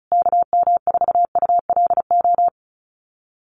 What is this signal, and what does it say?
Morse code, heard as a keyed tone, about 34 wpm, 700 Hz. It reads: KM4ULO
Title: USA Call Signs Repeated 3x 35wpm